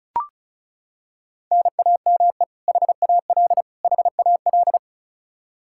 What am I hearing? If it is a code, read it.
Morse code, heard as a keyed tone, about 36 wpm, 700 Hz. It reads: NAME HAL HAL
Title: Practice QSO - Exchange 1 - Beginning - 35wpm